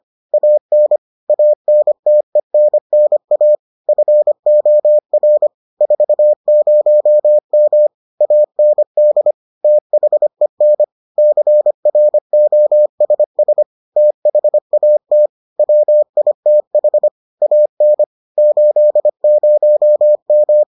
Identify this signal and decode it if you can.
Morse code, heard as a keyed tone, about 25 wpm, 600 Hz. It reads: AN ANTENNA FOR 40M AND THEN CROSS THAT WITH AN 80M